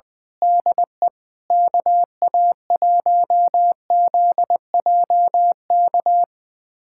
Morse code, heard as a keyed tone, about 21 wpm, 700 Hz. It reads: DE KA1ZJK
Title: USA Callsign DE Callsign - 6 Characters - 20wpm